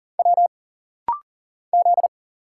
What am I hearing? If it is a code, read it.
Morse code, heard as a keyed tone, about 40 wpm, 700 Hz. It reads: W Z
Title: Single Letters - Rapid-Fire 40wpm